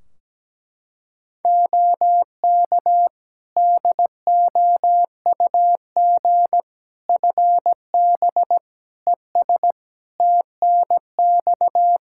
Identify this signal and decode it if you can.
Morse code, heard as a keyed tone, about 17 wpm, 700 Hz. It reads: OK DOUG FB ES TNX
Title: Practice QSO - Exchange 2 - Info - 17wpm